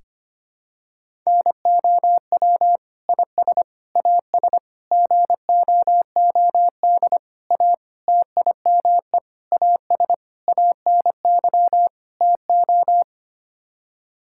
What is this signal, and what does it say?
Morse code, heard as a keyed tone, about 25 wpm, 700 Hz. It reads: NOW IS AS GOOD A TIME AS ANY TO
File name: Sets of 9 Words - 25wpm